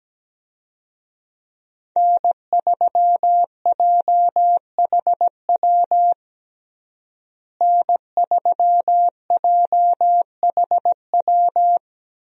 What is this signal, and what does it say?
Morse code, heard as a keyed tone, about 17 wpm, 700 Hz. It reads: N3JHW N3JHW
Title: USA Call Signs - 5 Characters - Repeated 2x - 17wpm